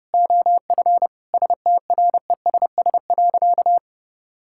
Morse code, heard as a keyed tone, about 30 wpm, 700 Hz. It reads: OF STRESS.
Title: Sentences from Top 1000 Words - Review Part 6 - 30wpm